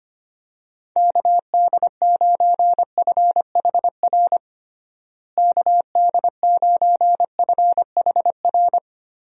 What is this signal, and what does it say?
Morse code, heard as a keyed tone, about 25 wpm, 700 Hz. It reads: KD9FHR KD9FHR
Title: USA Call Signs - 6 Characters - Repeated 2x - 25wpm